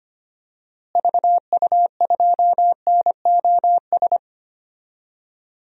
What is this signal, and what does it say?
Morse code, heard as a keyed tone, about 25 wpm, 700 Hz. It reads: VU2NOS
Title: DXCC-100 Call Signs - 6 Characters - Repeated 2x - 25wpm